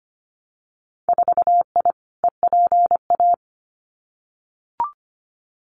Morse code, heard as a keyed tone, about 25 wpm, 700 Hz. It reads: 4I EPA
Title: Winter Field Day Exchanges 25wpm